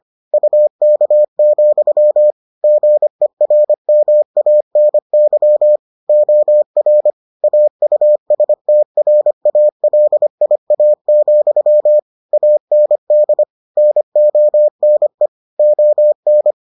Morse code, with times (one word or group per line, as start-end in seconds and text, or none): 0.3-2.3 UK,
2.6-5.8 GERMANY
6.1-7.1 OR
7.4-12.0 AUSTRALIA,
12.3-13.4 AND
13.8-15.3 NONE
15.6-16.5 ON